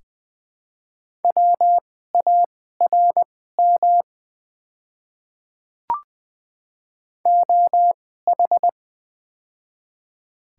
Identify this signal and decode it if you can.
Morse code, heard as a keyed tone, about 20 wpm, spaced as for 14 wpm, 700 Hz. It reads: WARM OH
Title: 2x Character Spacing - Top 500 Words - 20wpm